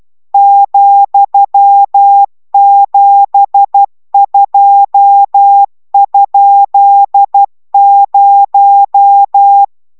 Morse code, heard as a keyed tone, about 12 wpm, 800 Hz. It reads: ,72?0